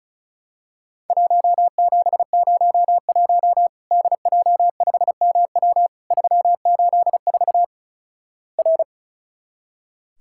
Morse code, heard as a keyed tone, about 36 wpm, 700 Hz. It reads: 1701 DJ5MW 384 R
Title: Worked All Europe DX Contest - QTC Exchange - 35wpm